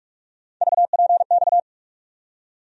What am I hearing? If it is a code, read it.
Morse code, heard as a keyed tone, about 46 wpm, 700 Hz. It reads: VPX
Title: USA Call Sign Postfixes - 3 Letters 45wpm